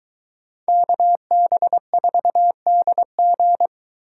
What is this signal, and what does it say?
Morse code, heard as a keyed tone, about 23 wpm, 700 Hz. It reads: KB4DG